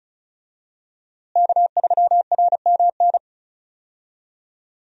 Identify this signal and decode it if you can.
Morse code, heard as a keyed tone, about 36 wpm, 700 Hz. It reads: K3RMN
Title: USA Call Signs - 5 Characters - Repeated 3x - 35wpm